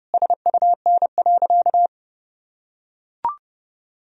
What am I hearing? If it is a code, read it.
Morse code, heard as a keyed tone, about 30 wpm, 700 Hz. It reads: SUN.
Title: Sentences from Top 1000 Words - Review Part 1 - 30wpm